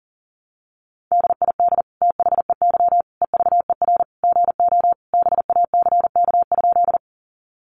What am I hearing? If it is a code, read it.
Morse code, heard as a keyed tone, about 40 wpm, 700 Hz. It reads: DID THEY EVER GO BACK?
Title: Sentences from Top 400 Words 40wpm